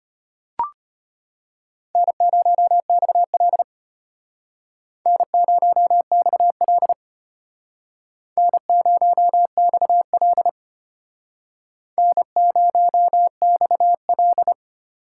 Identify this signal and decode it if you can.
Morse code, heard as a keyed tone, about 33 wpm, 700 Hz. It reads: N0XL N0XL N0XL N0XL